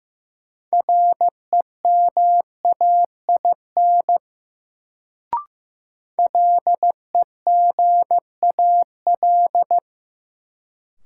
Morse code, heard as a keyed tone, about 15 wpm, 700 Hz. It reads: REMAIN LEGAL
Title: Sentences from Top 1200 Words - New Word Drill Alt - 15wpm